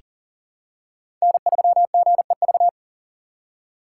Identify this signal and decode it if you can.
Morse code, heard as a keyed tone, about 43 wpm, 700 Hz. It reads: N3GEV
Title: Code Groups - 5 Letter:Numbers - 40wpm